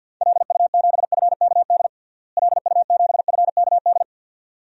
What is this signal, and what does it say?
Morse code, heard as a keyed tone, about 53 wpm, 700 Hz. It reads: LU7FKD LU7FKD
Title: DXCC-100 Call Signs - 6 Characters - Repeated 2x - 50wpm